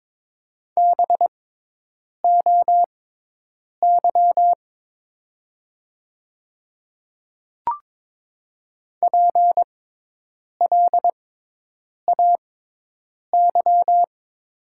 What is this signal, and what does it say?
Morse code, heard as a keyed tone, about 22 wpm, spaced as for 8 wpm, 700 Hz. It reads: BOY PLAY
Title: 6x Character Spacing - Top 500 Words - 22wpm